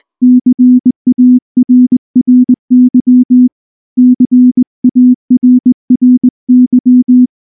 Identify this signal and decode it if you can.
Morse code, heard as a keyed tone, about 20 wpm, 250 Hz. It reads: CARRY CARRY
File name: morse / morse-training-20WPM-7